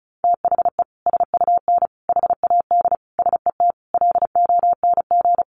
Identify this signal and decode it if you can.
Morse code, heard as a keyed tone, about 34 wpm, 700 Hz. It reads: THE SUN HAD SET LONG
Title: Sentences from Top 300 Words 35wpm